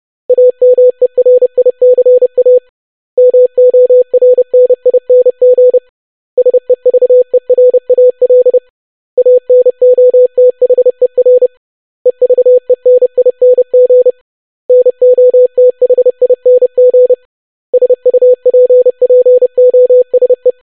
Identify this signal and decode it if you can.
Morse code, heard as a keyed tone, about 30 wpm, 500 Hz. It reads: AMERICA MORNING SEVERAL ANOTHER EVENING NOTHING SUPPOSE
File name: basic500-25wpm